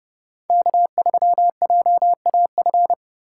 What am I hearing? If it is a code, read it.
Morse code, heard as a keyed tone, about 39 wpm, 700 Hz. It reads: K3JAF